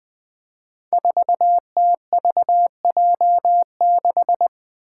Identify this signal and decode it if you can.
Morse code, heard as a keyed tone, about 21 wpm, 700 Hz. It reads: 4TVJ6
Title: Five Challenging Letter-Number Combinations 20wpm